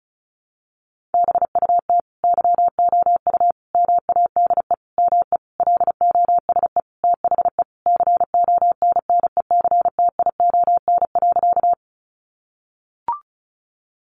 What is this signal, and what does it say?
Morse code, heard as a keyed tone, about 34 wpm, 700 Hz. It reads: BUT YOU MADE ME LOSE THE CONNECTION.
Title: Sentences from Top 1300 Words 35wpm